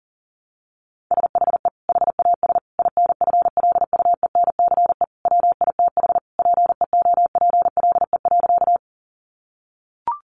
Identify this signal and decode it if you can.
Morse code, heard as a keyed tone, about 40 wpm, 700 Hz. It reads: SHE HAS INFLUENCE WITH PEOPLE.
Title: Sentences from Top 1100 Words 40wpm